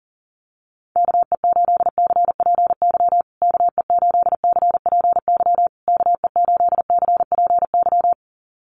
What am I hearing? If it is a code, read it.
Morse code, heard as a keyed tone, about 40 wpm, 700 Hz. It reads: KE8CPY KE8CPY KE8CPY
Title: USA Call Signs Repeated 3x 40wpm